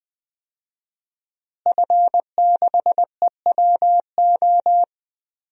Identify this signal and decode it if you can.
Morse code, heard as a keyed tone, about 20 wpm, 700 Hz. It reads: F6EWO
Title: DXCC-100 Call Signs - 5 Characters - Repeated 3x - 20wpm